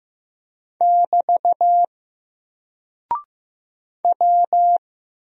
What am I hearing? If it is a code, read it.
Morse code, heard as a keyed tone, about 15 wpm, 700 Hz. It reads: <BT> W
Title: Single Character (Letter:Number:Punctuation:Prosign) 15wpm